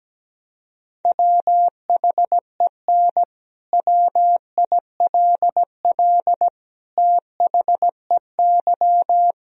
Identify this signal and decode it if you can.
Morse code, heard as a keyed tone, about 17 wpm, 700 Hz. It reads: WHEN WILL THEY
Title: Sentences from Top 1000 Words - Review Part 4 - 17wpm